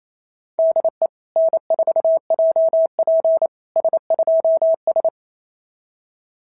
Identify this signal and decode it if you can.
Morse code, heard as a keyed tone, about 27 wpm, 660 Hz. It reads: DE N4JP S2S
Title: SOTA QSO Practice 28wpm